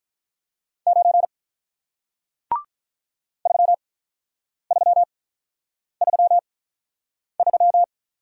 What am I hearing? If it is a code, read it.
Morse code, heard as a keyed tone, about 48 wpm, 700 Hz. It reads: 9 3 3 3 3